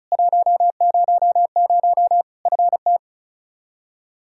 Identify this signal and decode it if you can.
Morse code, heard as a keyed tone, about 36 wpm, 700 Hz. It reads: 100 FT
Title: Practice QSO - Exchange 2 - Info - 35wpm